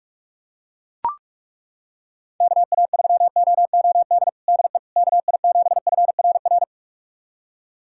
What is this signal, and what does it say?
Morse code, heard as a keyed tone, about 41 wpm, 700 Hz. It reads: KA3OOD DE KI7FRR
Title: USA Callsign DE Callsign 45wpm